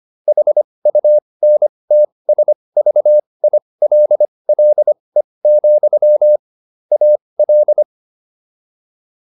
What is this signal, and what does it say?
Morse code, heard as a keyed tone, about 25 wpm, spaced as for 20 wpm, 600 Hz. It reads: HUNTSVILLE, AL